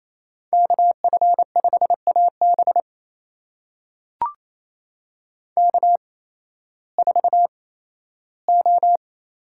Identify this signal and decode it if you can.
Morse code, heard as a keyed tone, about 28 wpm, spaced as for 15 wpm, 700 Hz. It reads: KF5AB K4O
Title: USA Call Signs - 8x Character Spacing - 28wpm